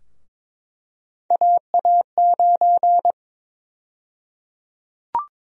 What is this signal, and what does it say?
Morse code, heard as a keyed tone, about 22 wpm, 700 Hz. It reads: AA9
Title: USA Call Sign Prefixes - 22wpm